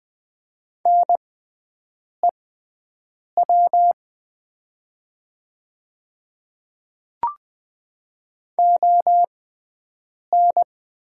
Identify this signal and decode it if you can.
Morse code, heard as a keyed tone, about 20 wpm, spaced as for 7 wpm, 700 Hz. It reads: NEW ON
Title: 6x Character Spacing - Top 200 Words - 20wpm